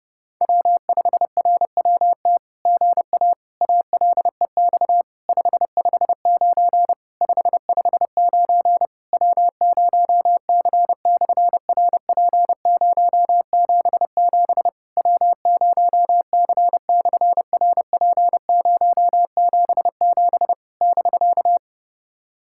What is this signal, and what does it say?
Morse code, heard as a keyed tone, about 30 wpm, 700 Hz. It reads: W5RWT GA ALEX 559 559 W0C/RP077 W0C/RP077 <BK>
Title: SOTA QSO Practice 30wpm